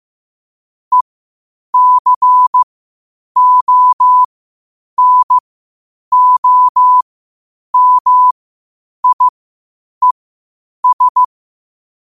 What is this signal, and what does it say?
Morse code, heard as a keyed tone, about 15 wpm, spaced as for 8 wpm, 1000 Hz. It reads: ECONOMIES
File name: qotdfiles / 8wpm